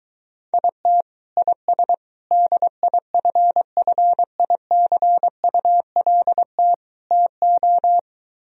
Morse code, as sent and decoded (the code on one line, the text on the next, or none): .. - / .. ... / -.. .. ..-. ..-. .. -.-. ..- .-.. - / - ---
IT IS DIFFICULT TO